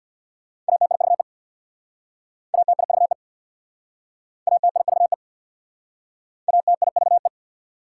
Morse code, heard as a keyed tone, about 66 wpm, 700 Hz. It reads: ATIVE ATIVE ATIVE ATIVE